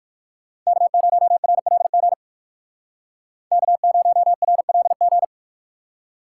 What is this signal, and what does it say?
Morse code, heard as a keyed tone, about 59 wpm, 700 Hz. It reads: K0RLG K0RLG